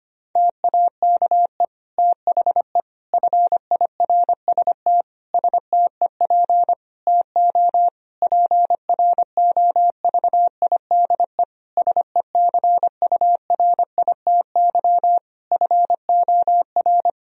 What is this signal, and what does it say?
Morse code, heard as a keyed tone, about 25 wpm, 700 Hz. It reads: TAKE THE FIRST STEP TO PROVIDE SECURITY FOR
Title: Sentences from Top 1000 Words - Review Part 4 - 25wpm